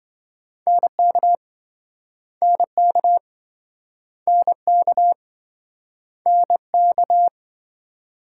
Morse code, heard as a keyed tone, about 27 wpm, 700 Hz. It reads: NK NK NK NK